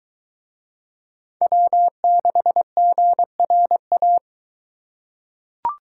Morse code, heard as a keyed tone, about 21 wpm, 700 Hz. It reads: W6GRA